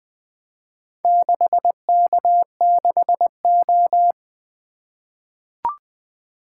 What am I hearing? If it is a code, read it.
Morse code, heard as a keyed tone, about 20 wpm, 700 Hz. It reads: 6K6O
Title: Four Letter-Number Combinations - 20wpm